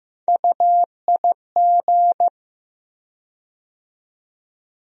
Morse code, heard as a keyed tone, about 15 wpm, 700 Hz. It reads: UIG
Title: USA Call Sign Postfixes - 3 Letters 15wpm